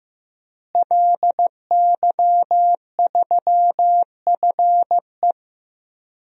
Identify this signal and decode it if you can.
Morse code, heard as a keyed tone, about 16 wpm, 700 Hz. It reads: LY3FE
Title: DXCC-100 Call Signs - 5 Characters - Repeated 3x - 15wpm